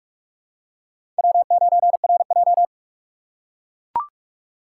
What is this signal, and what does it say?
Morse code, heard as a keyed tone, about 44 wpm, 700 Hz. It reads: W9RJ
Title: USA Call Signs - 4 Characters - 3x Character Spacing - 45wpm